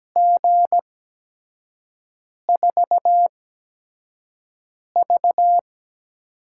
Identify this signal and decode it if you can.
Morse code, heard as a keyed tone, about 17 wpm, spaced as for 5 wpm, 700 Hz. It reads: G4V
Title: USA Call Signs - 5 Characters - 8x Character Spacing - 17wpm